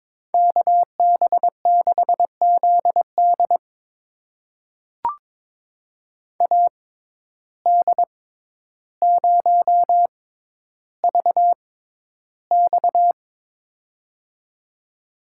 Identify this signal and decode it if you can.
Morse code, heard as a keyed tone, about 22 wpm, spaced as for 11 wpm, 700 Hz. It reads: KB6ZD AD0VX
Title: USA Call Signs - 5 Characters - 6x Character Spacing - 22wpm